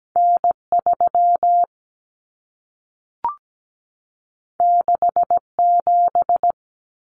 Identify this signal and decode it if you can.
Morse code, heard as a keyed tone, about 17 wpm, 700 Hz. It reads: N3 67
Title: Common Cut Numbers - 2-Digit 17wpm